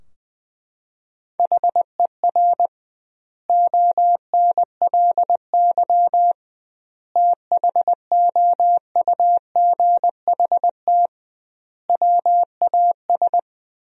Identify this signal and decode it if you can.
Morse code, heard as a keyed tone, about 20 wpm, 700 Hz. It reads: HER ONLY THOUGHT WAS
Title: 2x Word-Spacing - Sentences from Top 400 Words - 20wpm